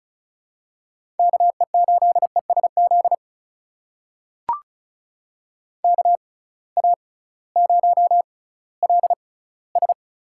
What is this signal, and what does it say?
Morse code, heard as a keyed tone, about 36 wpm, spaced as for 19 wpm, 700 Hz. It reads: KE8ESZ KA0LS
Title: USA Call Signs - 6x Character Spacing - 35wpm